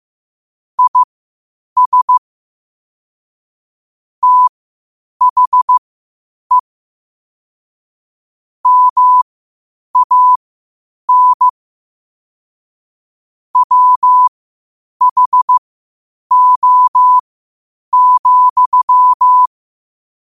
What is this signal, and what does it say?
Morse code, heard as a keyed tone, about 15 wpm, spaced as for 8 wpm, 1000 Hz. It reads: IS THE MAN WHO,
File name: qotdfiles / 8wpm